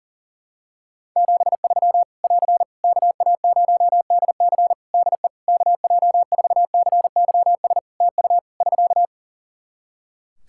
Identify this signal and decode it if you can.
Morse code, heard as a keyed tone, about 40 wpm, 700 Hz. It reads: 73 <AR> KA0DC DE KJ4CYS TU <SK>
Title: K1USN SST QSO Practice 40wpm